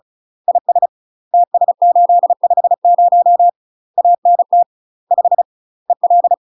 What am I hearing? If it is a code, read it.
Morse code, heard as a keyed tone, about 36 wpm, 700 Hz. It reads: IS TS850 ANT 5 EL
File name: UK MASTER QSOs TRAINING/35WPM